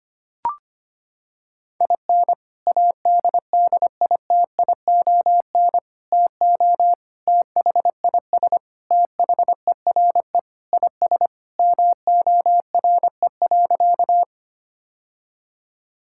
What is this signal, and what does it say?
Morse code, heard as a keyed tone, about 25 wpm, 700 Hz. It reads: IN ADDITION TO THIS THERE IS MORE.
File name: practice / 25wpm / Sentences with 3-Word Phrases - 25wpm